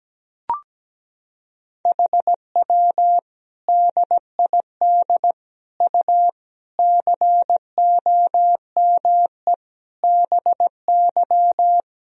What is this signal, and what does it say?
Morse code, heard as a keyed tone, about 17 wpm, 700 Hz. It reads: HW DID U COME BY
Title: Sentences from Top 100 Words and Abbreviations 17wpm